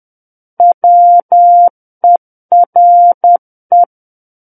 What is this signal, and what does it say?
Morse code, heard as a keyed tone, about 10 wpm, 700 Hz. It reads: WERE
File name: Book One / 01-Chapter 1 - 10 WPM